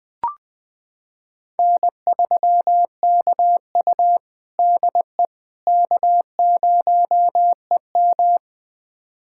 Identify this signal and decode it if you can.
Morse code, heard as a keyed tone, about 20 wpm, 700 Hz. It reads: N3KU DE K0EM
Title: USA Callsign DE Callsign - 4 Characters - 20wpm